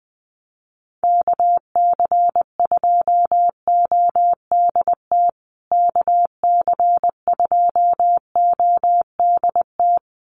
Morse code, heard as a keyed tone, about 20 wpm, 700 Hz. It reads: KC2ODT KC2ODT
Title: USA Call Signs Repeated 2x 20wpm